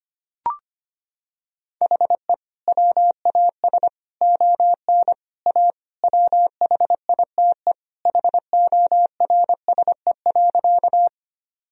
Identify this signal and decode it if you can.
Morse code, heard as a keyed tone, about 25 wpm, 700 Hz. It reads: HE WAS ON A WHITE HORSE.
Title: Sentences from Top 1000 Words - Review Part 1 - 25wpm